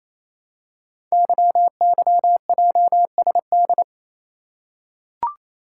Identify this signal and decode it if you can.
Morse code, heard as a keyed tone, about 28 wpm, 700 Hz. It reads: YYJSD
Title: Code Groups - 5 Letters - 28wpm